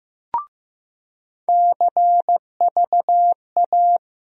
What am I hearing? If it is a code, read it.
Morse code, heard as a keyed tone, about 15 wpm, 700 Hz. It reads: CVA
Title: USA Call Sign Postfixes - 3 Letters 15wpm